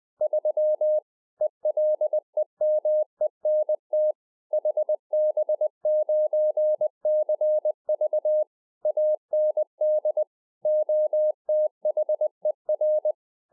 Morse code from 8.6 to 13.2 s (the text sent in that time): AND OTHER